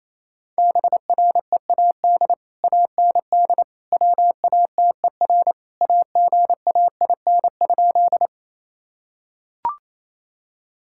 Morse code, heard as a keyed tone, about 28 wpm, 700 Hz. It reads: BREAD AND WATER AGAIN?
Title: Sentences from Top 1600 Words 28wpm